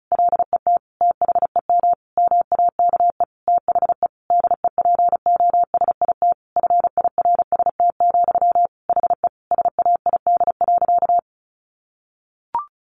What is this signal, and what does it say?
Morse code, heard as a keyed tone, about 34 wpm, 700 Hz. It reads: LET THEM MAKE THE DEPOSIT FIRST, HE SAID.
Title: Sentences from Top 1300 Words 35wpm